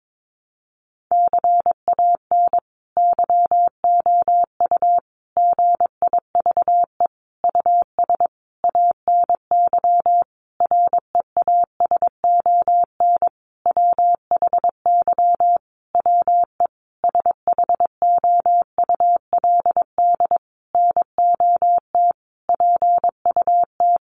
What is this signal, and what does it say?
Morse code, heard as a keyed tone, about 22 wpm, 700 Hz. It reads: CAN YOU GIVE US ANY REASON WHY WE SHOULD NOT PUT
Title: Sentences from Top 1100 Words 22wpm